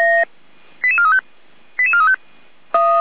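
Five-tone selective call (ZVEI), heard as quick runs of five tones, tones of about 70 ms. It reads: 88435 88435
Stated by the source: ZVEI five-tone selective call received on an SDR, recorded by Feuerwehr Orlamünde.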